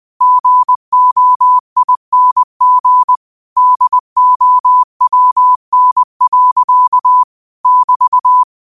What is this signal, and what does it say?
Morse code, heard as a keyed tone, about 20 wpm, 1000 Hz. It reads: GOING DOWN. <BT>